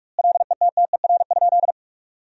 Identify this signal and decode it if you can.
Morse code, heard as a keyed tone, about 42 wpm, 700 Hz. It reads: LETTER?
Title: 2x Word-Spacing - Sentences from Top 300 Words - 45wpm